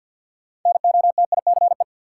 Morse code, heard as a keyed tone, about 53 wpm, 700 Hz. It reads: NOTICE
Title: Sentences from Top 400 Words - New Word Drill - 50wpm